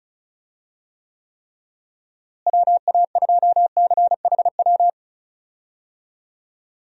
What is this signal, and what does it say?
Morse code, heard as a keyed tone, about 36 wpm, 700 Hz. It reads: WA2CHW
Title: USA Call Signs - 6 Characters - Repeated 3x - 35wpm